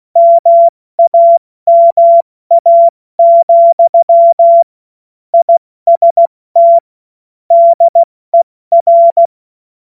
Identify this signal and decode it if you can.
Morse code, heard as a keyed tone, about 16 wpm, 680 Hz. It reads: MAMA, IST DER
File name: Morsekurs / Witze / Witz 14. März 2023 70 F80